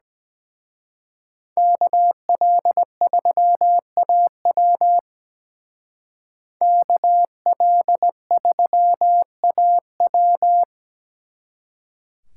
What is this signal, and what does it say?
Morse code, heard as a keyed tone, about 20 wpm, 700 Hz. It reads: KL3AW KL3AW
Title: Speed Racing - USA 2x2 Call Signs 17wpm